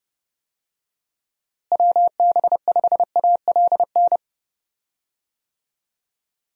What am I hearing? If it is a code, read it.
Morse code, heard as a keyed tone, about 32 wpm, 700 Hz. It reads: WB5ALN
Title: USA Call Signs - 6 Characters - Repeated 2x - 30wpm